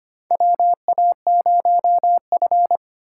Morse code, heard as a keyed tone, about 25 wpm, 700 Hz. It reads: WA0F
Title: USA Call Signs - 4 Characters - 2x Character Spacing - 25wpm